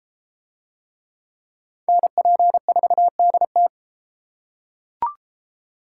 Morse code, heard as a keyed tone, about 32 wpm, 700 Hz. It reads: NP4DT